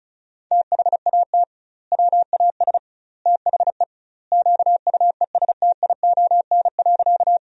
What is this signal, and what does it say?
Morse code, heard as a keyed tone, about 36 wpm, 700 Hz. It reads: THAT WAS THE QUESTION.
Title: 2x Word-Spacing - Sentences from Top 400 Words - 35wpm